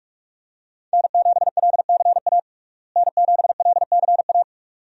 Morse code, heard as a keyed tone, about 46 wpm, 700 Hz. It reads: N7LCA N7LCA
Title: USA Call Signs - 5 Characters - Repeated 2x - 45wpm